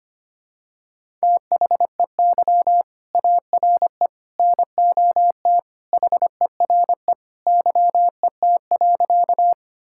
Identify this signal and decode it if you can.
Morse code, heard as a keyed tone, about 25 wpm, 700 Hz. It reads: THEY ARE NOT HERE YET.
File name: practice / 25wpm / Sentences with 2-Word Phrases - 25wpm